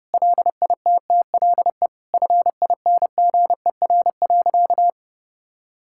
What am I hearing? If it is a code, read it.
Morse code, heard as a keyed tone, about 32 wpm, 700 Hz. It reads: LITTLE FINGER.
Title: Sentences from Top 1600 Words 30wpm